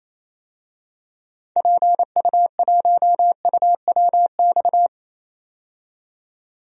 Morse code, heard as a keyed tone, about 28 wpm, 700 Hz. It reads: PU1UWX
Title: DXCC-100 Call Signs - 6 Characters - 28wpm